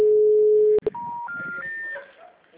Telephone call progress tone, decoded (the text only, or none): special information tone